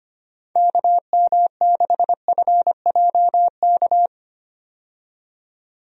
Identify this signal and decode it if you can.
Morse code, heard as a keyed tone, about 25 wpm, 700 Hz. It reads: KM6FJK
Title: USA Call Signs - 6 Characters - 25wpm